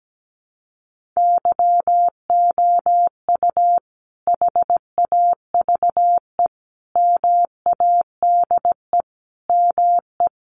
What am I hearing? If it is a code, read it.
Morse code, heard as a keyed tone, about 17 wpm, 700 Hz. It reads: YOU HAVE MADE ME
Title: Sentences from Top 1300 Words 17wpm